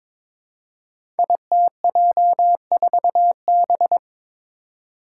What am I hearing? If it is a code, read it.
Morse code, heard as a keyed tone, about 23 wpm, 700 Hz. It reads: ITJ4B
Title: Five Challenging Letter-Number Combinations 22wpm